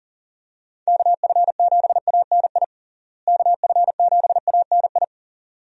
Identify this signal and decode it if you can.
Morse code, heard as a keyed tone, about 40 wpm, 700 Hz. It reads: KF7ANI KF7ANI
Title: USA Call Signs - 6 Characters - Repeated 2x - 40wpm